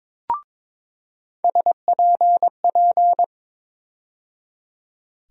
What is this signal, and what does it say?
Morse code, heard as a keyed tone, about 22 wpm, 700 Hz. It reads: SPP
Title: USA Call Sign Postfixes - 3 Letters 22wpm